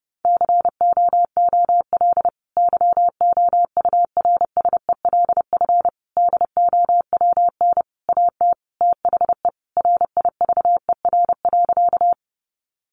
Morse code, heard as a keyed tone, about 30 wpm, 700 Hz. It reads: COOL YOURSELF DOWN AT THE RIVER.
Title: Sentences from Top 1200 Words 30wpm